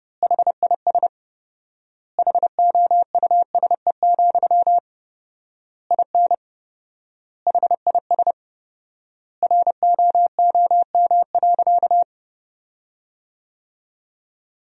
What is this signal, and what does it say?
Morse code, heard as a keyed tone, about 30 wpm, 700 Hz. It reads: HIS HOUSE, IN HIS ROOM.
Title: 4x Word-Spacing - Sentences from Top 300 Words - 30wpm